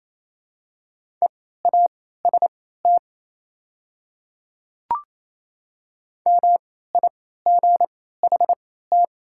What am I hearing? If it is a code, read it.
Morse code, heard as a keyed tone, about 28 wpm, spaced as for 16 wpm, 690 Hz. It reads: EAST MIGHT
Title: 3x Character Spacing - Top 500 Words - 28wpm